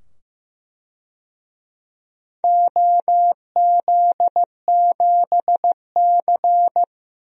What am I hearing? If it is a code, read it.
Morse code, heard as a keyed tone, about 15 wpm, 700 Hz. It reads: OZ7C
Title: DXCC-100 Call Signs - 4 Characters - 15wpm